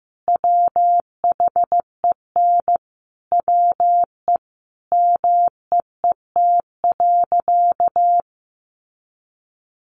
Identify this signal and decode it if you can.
Morse code, heard as a keyed tone, about 15 wpm, 700 Hz. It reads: WHEN WE MEET.
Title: Sentences from Top 1500 Words 15wpm